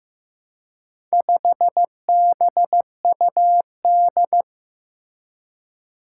Morse code, heard as a keyed tone, about 15 wpm, 700 Hz. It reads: 5BUD
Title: Four Challenging Letter-Number Combinations 15wpm